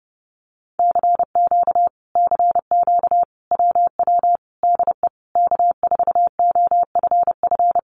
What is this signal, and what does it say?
Morse code, heard as a keyed tone, about 30 wpm, 700 Hz. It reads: CQ CQ WW DE K4OFF
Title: Common QSO Phrases Encore 30wpm